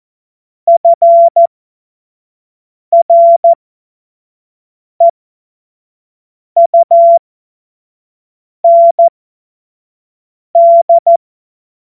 Morse code, heard as a keyed tone, about 14 wpm, spaced as for 5 wpm, 680 Hz. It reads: FREUND
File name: Witz-05-August-2025-25-F70